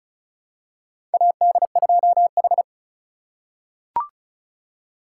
Morse code, heard as a keyed tone, about 34 wpm, 700 Hz. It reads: AD2H
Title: USA Call Signs - 4 Characters - 2x Character Spacing - 35wpm